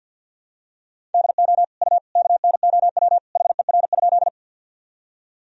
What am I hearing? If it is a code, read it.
Morse code, heard as a keyed tone, about 51 wpm, 700 Hz. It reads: DO U KNOW HER?
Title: Sentences from Top 100 Words and Abbreviations 50wpm